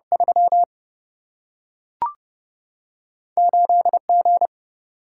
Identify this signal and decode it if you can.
Morse code, heard as a keyed tone, about 30 wpm, 700 Hz. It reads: 3 8 G
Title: Single-Letter Single-Number 30wpm